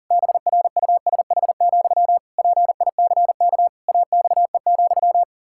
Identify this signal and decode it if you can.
Morse code, heard as a keyed tone, about 40 wpm, 700 Hz. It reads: BRUSH, PICK AXE,
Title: Pangram Challenge v2 - 40wpm